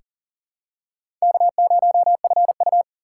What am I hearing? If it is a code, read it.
Morse code, heard as a keyed tone, about 40 wpm, 690 Hz. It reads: K0FU
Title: USA Call Signs - 4 Characters - 2x Character Spacing - 40wpm